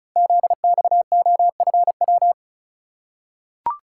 Morse code, heard as a keyed tone, about 35 wpm, 700 Hz. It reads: ZXOFW
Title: Code Groups - 5 Letter:Numbers - 35wpm